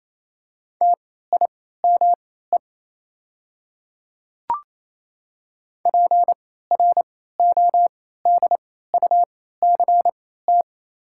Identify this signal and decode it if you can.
Morse code, heard as a keyed tone, about 28 wpm, spaced as for 16 wpm, 700 Hz. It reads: TIME PRODUCT
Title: 3x Character Spacing - Top 500 Words - 28wpm